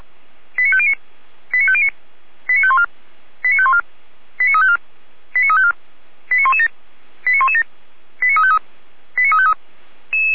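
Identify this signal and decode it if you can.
Five-tone selective call (ZVEI), heard as quick runs of five tones, tones of about 80 ms. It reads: 89509 89509 89514 89514 89354 89354 89117 89117 89353 89353
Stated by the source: SDR-received ZVEI five-tone selective call, code 89164, recorded by feuerwehr-orlamuende fire brigade.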